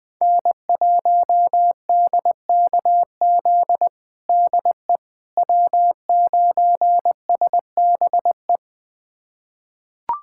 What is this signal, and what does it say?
Morse code, heard as a keyed tone, about 20 wpm, 700 Hz. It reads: N1DKZ DE W9SBE
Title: USA Callsign DE Callsign - 5 Characters - 20wpm